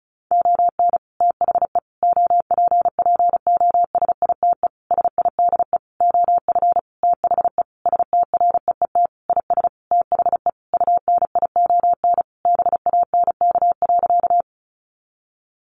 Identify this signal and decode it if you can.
Morse code, heard as a keyed tone, about 34 wpm, 700 Hz. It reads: ON THE OPPOSITE SIDE OF THE STREET IS THE UNION BANK.
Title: Sentences from Top 1400 Words 35wpm